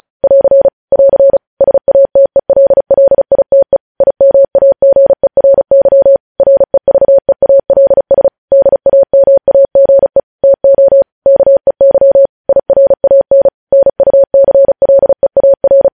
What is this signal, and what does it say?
Morse code, heard as a keyed tone, about 36 wpm, 550 Hz. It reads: <AR> <AR> SATELLITE IMAGERY REVEALS DAMAGE TO KEY IRAN NUCLEAR
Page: news headlines in morse code at 35 wpm